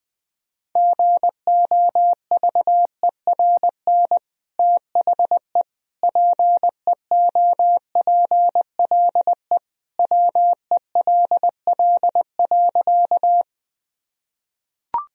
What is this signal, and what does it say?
Morse code, heard as a keyed tone, about 20 wpm, 700 Hz. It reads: GOVERN THE PEOPLE WELL.
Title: Sentences from Top 1000 Words - Review Part 2 - 20wpm